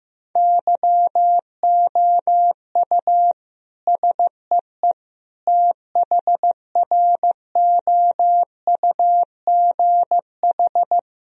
Morse code, heard as a keyed tone, about 16 wpm, 700 Hz. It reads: YOU SEE THROUGH
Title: Sentences from Top 1000 Words - Review Part 1 - 15wpm